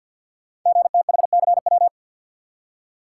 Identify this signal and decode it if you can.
Morse code, heard as a keyed tone, about 50 wpm, 700 Hz. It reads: GTHCW
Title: Code Groups - 5 Letters - 50wpm